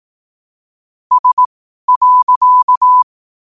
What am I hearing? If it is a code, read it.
Morse code, heard as a keyed tone, about 18 wpm, spaced as for 12 wpm, 1000 Hz. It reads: S.
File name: qotdfiles / 12wpm